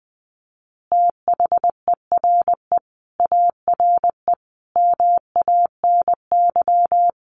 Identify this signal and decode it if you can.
Morse code, heard as a keyed tone, about 20 wpm, 700 Hz. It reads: THERE ARE MANY
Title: Easy Phrases 20wpm